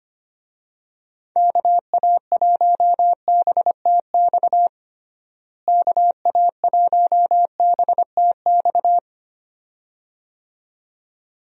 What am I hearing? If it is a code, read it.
Morse code, heard as a keyed tone, about 25 wpm, 700 Hz. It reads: KA1BTX KA1BTX
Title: USA Call Signs - 6 Characters - Repeated 2x - 25wpm